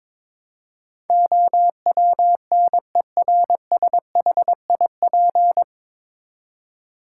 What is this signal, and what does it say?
Morse code, heard as a keyed tone, about 22 wpm, 700 Hz. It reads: OWNERSHIP
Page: Morse Code Ninja practice 22wpm